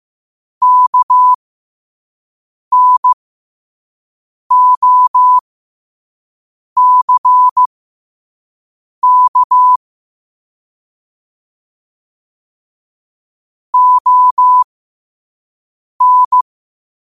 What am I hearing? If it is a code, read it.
Morse code, heard as a keyed tone, about 15 wpm, spaced as for 5 wpm, 1000 Hz. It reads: KNOCK ON
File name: qotdfiles / 5wpm